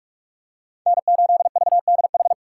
Morse code, heard as a keyed tone, about 44 wpm, 690 Hz. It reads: N8VDH
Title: USA Call Signs - 5 Characters - 6x Character Spacing - 45wpm